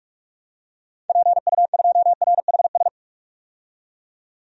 Morse code, heard as a keyed tone, about 42 wpm, 700 Hz. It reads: PU2RHS
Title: DXCC-100 Call Signs - 6 Characters - 45wpm